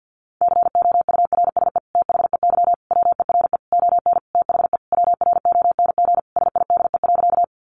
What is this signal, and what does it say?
Morse code, heard as a keyed tone, about 50 wpm, 700 Hz. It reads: COURSE THEY WERE ON THE WRONG SIDE.
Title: Sentences from Top 900 Words 50wpm